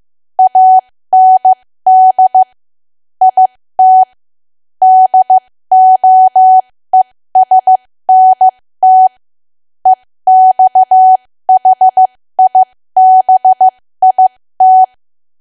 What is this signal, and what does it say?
Morse code, heard as a keyed tone, about 15 wpm, spaced as for 13 wpm, 750 Hz. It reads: AND IT DOESNT EXHIBIT
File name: cwa / W1AW-Level2 / 13WPM